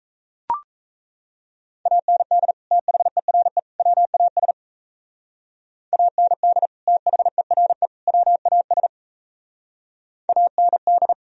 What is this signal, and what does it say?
Morse code, heard as a keyed tone, about 35 wpm, 700 Hz. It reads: AND THERE WAS AND THERE WAS AND